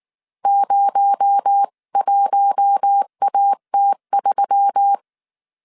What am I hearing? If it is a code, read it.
Morse code, heard as a keyed tone, about 19 wpm, 800 Hz. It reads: 01AT3
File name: CW / DET / 1 6 1500C S C 18wpm 2356